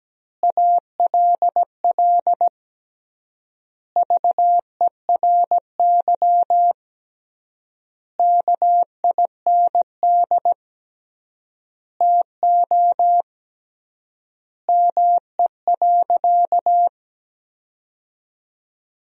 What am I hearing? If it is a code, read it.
Morse code, heard as a keyed tone, about 17 wpm, 700 Hz. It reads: ALL VERY KIND TO ME.
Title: 3x Word-Spacing - Sentences from Top 200 Words - 17wpm